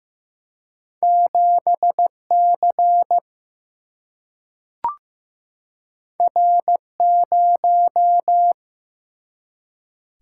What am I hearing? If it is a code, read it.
Morse code, heard as a keyed tone, about 15 wpm, 700 Hz. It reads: 7C R0
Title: Single-Letter Single-Number 15wpm